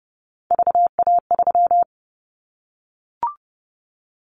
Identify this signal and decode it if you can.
Morse code, heard as a keyed tone, about 30 wpm, 700 Hz. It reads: VA3
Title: Callsign Prefixes for Top 50 DXCC - 30wpm